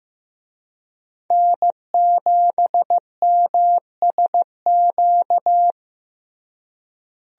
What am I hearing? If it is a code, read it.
Morse code, heard as a keyed tone, about 15 wpm, 700 Hz. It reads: N7MSQ
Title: USA Call Signs - 5 Characters - Repeated 3x - 15wpm